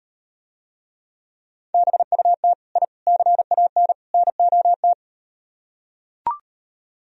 Morse code, heard as a keyed tone, about 38 wpm, 700 Hz. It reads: BUT I CAN NOT